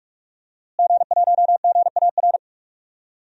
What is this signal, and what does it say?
Morse code, heard as a keyed tone, about 46 wpm, 700 Hz. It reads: G1GAR
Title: DXCC-100 Call Signs - 5 Characters - Repeated 2x - 45wpm